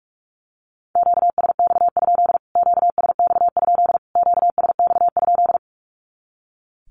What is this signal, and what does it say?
Morse code, heard as a keyed tone, about 44 wpm, 700 Hz. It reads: QSX? QSX? QSX?
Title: Q-Codes Repeated 3x 45wpm